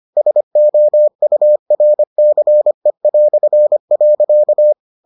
Morse code, heard as a keyed tone, about 25 wpm, 600 Hz. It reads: SOURCE".